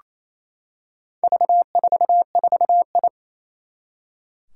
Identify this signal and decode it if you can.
Morse code, heard as a keyed tone, about 27 wpm, 700 Hz. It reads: V44I
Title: Four Challenging Letter-Number Combinations 28wpm